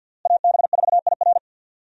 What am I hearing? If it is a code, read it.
Morse code, heard as a keyed tone, about 51 wpm, 700 Hz. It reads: AB4IR
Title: USA Call Signs - 5 Characters - Repeated 2x - 50wpm